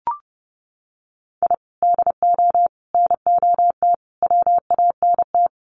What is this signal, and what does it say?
Morse code, heard as a keyed tone, about 30 wpm, 700 Hz. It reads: I DO NOT WANT
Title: Sets of 5 Words 30wpm